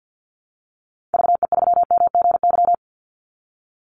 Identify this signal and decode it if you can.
Morse code, heard as a keyed tone, about 50 wpm, 700 Hz. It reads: VE3MGY
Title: Call Signs Repeated 3x - 50wpm